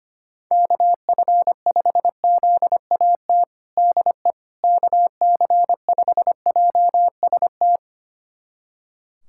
Morse code, heard as a keyed tone, about 25 wpm, 700 Hz. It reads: KF5ZAT DE KC5JST
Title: USA Callsign DE Callsign - 6 Characters - 25wpm